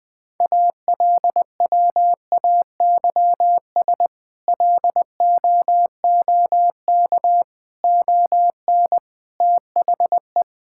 Morse code, heard as a keyed tone, about 20 wpm, 700 Hz. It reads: ALWAYS LOOK ON THE
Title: Sentences with 8-Word Phrases - Spoken Beginning - 20wpm